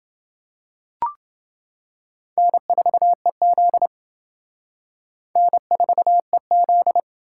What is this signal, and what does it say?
Morse code, heard as a keyed tone, about 28 wpm, 700 Hz. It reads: N4EZ N4EZ